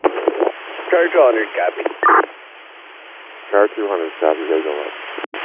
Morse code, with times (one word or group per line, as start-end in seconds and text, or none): none